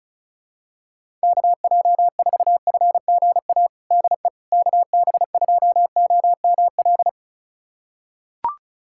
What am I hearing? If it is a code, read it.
Morse code, heard as a keyed tone, about 36 wpm, 700 Hz. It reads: KJ4FGA DE KB2OML
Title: USA Callsign DE Callsign - 6 Characters - 35wpm